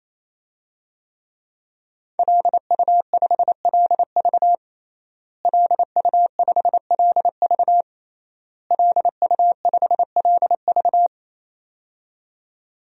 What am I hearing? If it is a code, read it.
Morse code, heard as a keyed tone, about 27 wpm, 700 Hz. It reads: LU5LV LU5LV LU5LV
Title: DXCC-100 Call Signs - 5 Characters - Repeated 3x - 28wpm